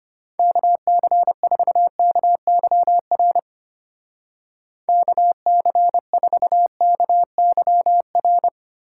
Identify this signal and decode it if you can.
Morse code, heard as a keyed tone, about 32 wpm, 700 Hz. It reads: KC4KYR KC4KYR